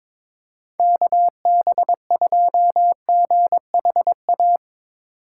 Morse code, heard as a keyed tone, about 22 wpm, 700 Hz. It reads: KB2GHA
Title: USA Call Signs - 6 Characters - 3x Character Spacing - 22wpm